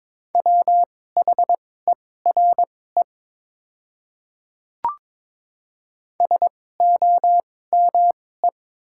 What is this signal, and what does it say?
Morse code, heard as a keyed tone, about 22 wpm, spaced as for 16 wpm, 700 Hz. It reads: WHERE SOME
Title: 2x Character Spacing - Top 200 Words - 22wpm